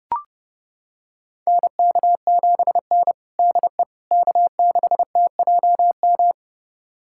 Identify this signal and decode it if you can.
Morse code, heard as a keyed tone, about 32 wpm, 700 Hz. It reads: NK7N DE K6TJM
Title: USA Callsign DE Callsign 30wpm